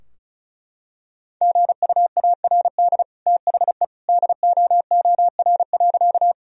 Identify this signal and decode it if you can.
Morse code, heard as a keyed tone, about 34 wpm, 700 Hz. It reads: GUARD THE DOOR.
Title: Sentences from Top 1000 Words - Review Part 4 - 35wpm